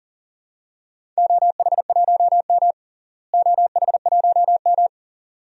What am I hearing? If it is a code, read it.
Morse code, heard as a keyed tone, about 40 wpm, 700 Hz. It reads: OH1M OH1M
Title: DXCC-100 Call Signs - 4 Characters - Repeated 3x - 40wpm